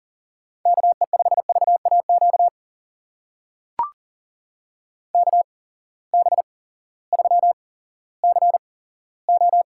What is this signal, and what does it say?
Morse code, heard as a keyed tone, about 40 wpm, spaced as for 18 wpm, 700 Hz. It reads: KE5VAQ KB3CO
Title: USA Call Signs - 6 Characters - 8x Character Spacing - 40wpm